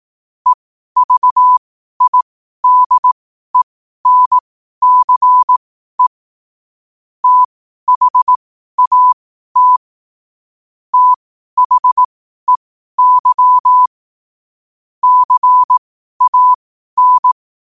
Morse code, heard as a keyed tone, about 18 wpm, spaced as for 12 wpm, 1000 Hz. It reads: EVIDENCE THAT THEY CAN
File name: qotdfiles / 12wpm